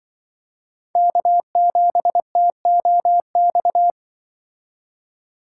Morse code, heard as a keyed tone, about 24 wpm, 700 Hz. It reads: K7TOX
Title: Speed Racing - USA 1x3 Call Signs 20wpm